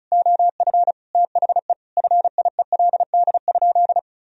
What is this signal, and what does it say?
Morse code, heard as a keyed tone, about 36 wpm, 700 Hz. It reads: OF THE FIELD?
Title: Sentences from Top 1000 Words - Review Part 2 - 35wpm